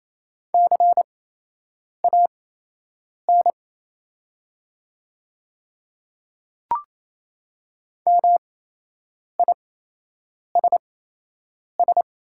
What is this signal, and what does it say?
Morse code, heard as a keyed tone, about 28 wpm, spaced as for 8 wpm, 700 Hz. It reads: CAN MISS
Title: 8x Character Spacing - Top 500 Words - 28wpm